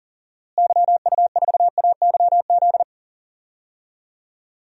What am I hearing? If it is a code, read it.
Morse code, heard as a keyed tone, about 40 wpm, 700 Hz. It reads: YU4AYZ
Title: DXCC-100 Call Signs - 6 Characters - Repeated 3x - 40wpm